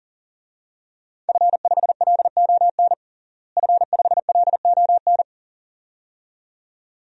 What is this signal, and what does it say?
Morse code, heard as a keyed tone, about 37 wpm, 700 Hz. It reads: F5LON F5LON
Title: DXCC-100 Call Signs - 5 Characters - Repeated 2x - 40wpm